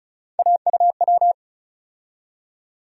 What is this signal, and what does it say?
Morse code, heard as a keyed tone, about 36 wpm, 700 Hz. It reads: AUW
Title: USA Call Sign Postfixes - 3 Letters 35wpm